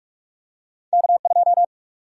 Morse code, heard as a keyed tone, about 45 wpm, 700 Hz. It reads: K2
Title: USA Call Sign Prefixes - 45wpm